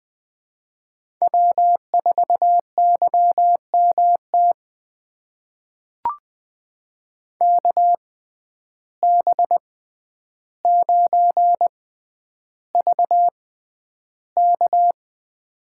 Morse code, heard as a keyed tone, about 20 wpm, spaced as for 10 wpm, 700 Hz. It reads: W4YMT KB9VK
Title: USA Call Signs - 5 Characters - 6x Character Spacing - 20wpm